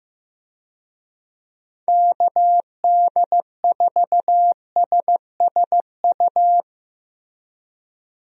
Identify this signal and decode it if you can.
Morse code, heard as a keyed tone, about 15 wpm, 700 Hz. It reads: KD4SSU
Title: USA Call Signs - 6 Characters - Repeated 3x - 15wpm